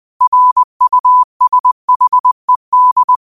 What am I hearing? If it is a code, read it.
Morse code, heard as a keyed tone, about 20 wpm, 1000 Hz. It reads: RUSHED
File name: qotdfiles / 20wpm